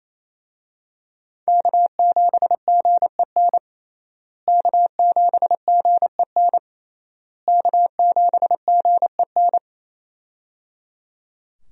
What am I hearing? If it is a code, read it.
Morse code, heard as a keyed tone, about 28 wpm, 700 Hz. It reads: K7GEN K7GEN K7GEN
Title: USA Call Signs - 5 Characters - Repeated 3x - 28wpm